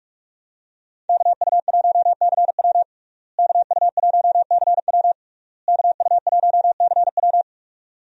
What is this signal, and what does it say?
Morse code, heard as a keyed tone, about 46 wpm, 700 Hz. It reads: KU1CW KU1CW KU1CW
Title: USA Call Signs - 5 Characters - Repeated 3x - 45wpm